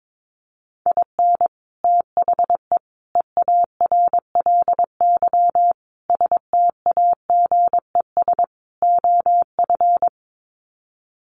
Morse code, heard as a keyed tone, about 22 wpm, 700 Hz. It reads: IN THE EARLY STAGES OF
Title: Sets of 5 Words Encore 22wpm